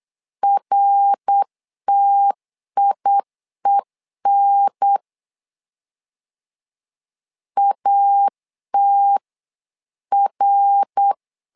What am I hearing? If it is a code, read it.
Morse code, heard as a keyed tone, about 8 wpm, 800 Hz. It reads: RTIEN ATR
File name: LEZIONI / 3 MWRO / 3 MWR 8wpm 2444